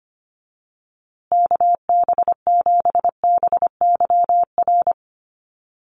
Morse code, heard as a keyed tone, about 25 wpm, 700 Hz. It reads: KB7BYR
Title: USA Call Signs Repeated 2x 25wpm